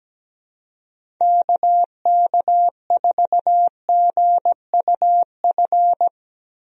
Morse code, heard as a keyed tone, about 17 wpm, 700 Hz. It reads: KK4GUF
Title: USA Call Signs - 6 Characters - 17wpm